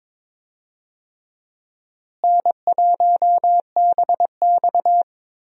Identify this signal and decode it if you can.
Morse code, heard as a keyed tone, about 22 wpm, 700 Hz. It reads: N1BX
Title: USA Call Signs - 4 Characters - Repeated 3x - 22wpm